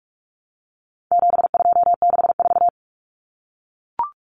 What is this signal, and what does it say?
Morse code, heard as a keyed tone, about 44 wpm, 700 Hz. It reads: 7264
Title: Numbers - 4-Digit 45wpm